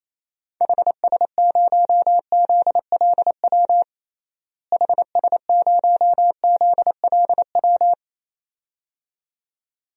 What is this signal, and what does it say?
Morse code, heard as a keyed tone, about 28 wpm, 700 Hz. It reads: HS0ZLW HS0ZLW
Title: DXCC-100 Call Signs - 6 Characters - Repeated 2x - 28wpm